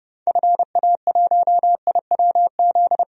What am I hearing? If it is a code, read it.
Morse code, heard as a keyed tone, about 30 wpm, 700 Hz. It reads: FA1IWZ
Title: DXCC-100 Call Signs - 6 Characters - Repeated 3x - 30wpm